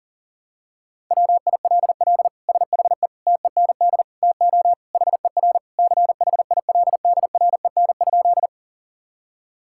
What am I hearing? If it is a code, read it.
Morse code, heard as a keyed tone, about 40 wpm, 700 Hz. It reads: WILL SHE TEND TO HER CHILDREN?
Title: Sentences from Top 1000 Words - Review Part 5 - 40wpm